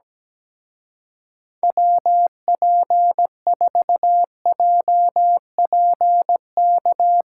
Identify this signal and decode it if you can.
Morse code, heard as a keyed tone, about 17 wpm, 700 Hz. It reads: WP4JPK
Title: Speed Racing - USA 2x3 Call Signs 17wpm